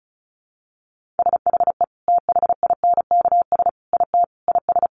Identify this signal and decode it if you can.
Morse code, heard as a keyed tone, about 34 wpm, 700 Hz. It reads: SHE THINKS IT IS
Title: Easy Phrases 35wpm